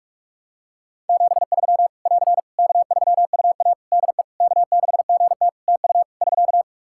Morse code, heard as a keyed tone, about 48 wpm, 700 Hz. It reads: 73 <AR> K3UA DE K6GT TU <SK>
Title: Practice QSO - Exchange 4 - Ending - 45wpm